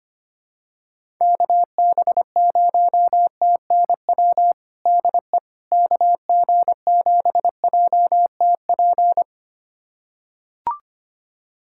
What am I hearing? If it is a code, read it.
Morse code, heard as a keyed tone, about 25 wpm, 700 Hz. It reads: KB0TNW DE KG7JTP
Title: USA Callsign DE Callsign - 6 Characters - 25wpm